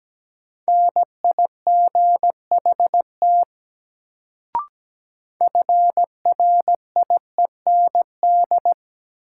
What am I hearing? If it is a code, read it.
Morse code, heard as a keyed tone, about 17 wpm, 700 Hz. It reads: NIGHT FRIEND
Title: Sentences from Top 300 Words - New Word Drill - 17wpm